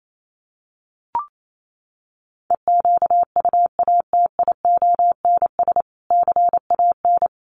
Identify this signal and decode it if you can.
Morse code, heard as a keyed tone, about 28 wpm, 700 Hz. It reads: EQUATIONS CAN
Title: Sets of 5 Words 28wpm